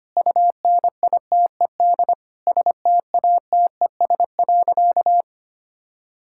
Morse code, heard as a keyed tone, about 25 wpm, 700 Hz. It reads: UNITED STATES.
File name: practice / 25wpm / Sentences with 2-Word Phrases - 25wpm